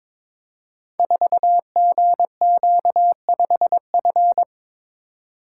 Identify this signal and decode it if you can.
Morse code, heard as a keyed tone, about 23 wpm, 700 Hz. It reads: 4GQ5F
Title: Code Groups - 5 Letter:Numbers - 22wpm